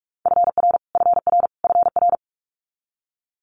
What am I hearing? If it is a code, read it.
Morse code, heard as a keyed tone, about 46 wpm, 700 Hz. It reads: FR FR FR
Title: Multi-letter Phonemes Repeated 3x 45wpm